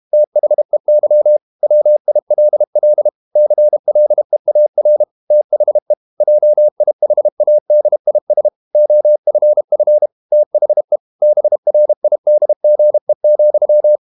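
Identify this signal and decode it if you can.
Morse code, heard as a keyed tone, about 32 wpm, 600 Hz. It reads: THEY WILL CLEAR THE JIHADIS OFF THE BRIDGE,
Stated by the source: Morse code practice at 32 wpm from CW med Gnister, reading Ken Follett's novel Never in English.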